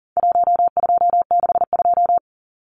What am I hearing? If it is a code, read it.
Morse code, heard as a keyed tone, about 40 wpm, 700 Hz. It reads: 1262
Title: Numbers - 4-Digit 40wpm